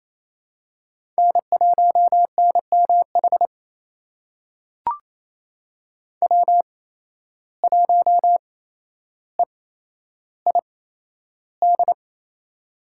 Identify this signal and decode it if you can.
Morse code, heard as a keyed tone, about 28 wpm, spaced as for 12 wpm, 700 Hz. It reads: N1NMH W1EID
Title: USA Call Signs - 5 Characters - 8x Character Spacing - 28wpm